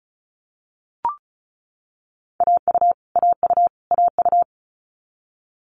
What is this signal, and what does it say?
Morse code, heard as a keyed tone, about 36 wpm, 700 Hz. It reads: AU AU AU
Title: Multi-letter Phonemes Repeated 3x 35wpm